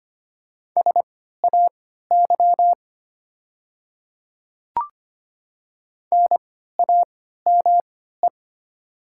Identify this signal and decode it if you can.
Morse code, heard as a keyed tone, about 25 wpm, spaced as for 14 wpm, 700 Hz. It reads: SAY NAME
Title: 3x Character Spacing - Top 300 Words - 25wpm